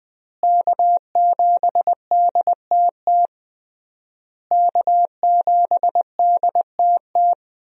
Morse code, heard as a keyed tone, about 21 wpm, 700 Hz. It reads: K7DTT K7DTT
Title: USA Call Signs - 5 Characters - Repeated 3x - 20wpm